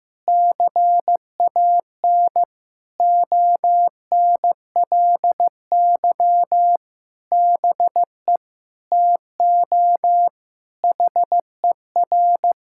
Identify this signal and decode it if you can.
Morse code, heard as a keyed tone, about 14 wpm, 700 Hz. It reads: CAN ONLY BE TO HER
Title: Sentences from Top 1000 Words - Review Part 6 - 15wpm